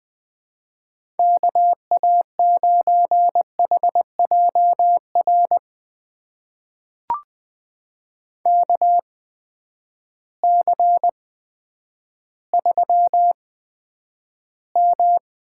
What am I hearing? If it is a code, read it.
Morse code, heard as a keyed tone, about 20 wpm, spaced as for 10 wpm, 700 Hz. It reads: KA9HJR KC3M
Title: USA Call Signs - 6 Characters - 8x Character Spacing - 20wpm